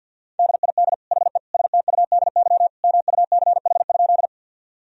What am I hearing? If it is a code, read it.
Morse code, heard as a keyed tone, about 50 wpm, 700 Hz. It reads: DID HE STUDY MUCH?
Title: Sentences from Top 1000 Words - Review Part 1 - 50wpm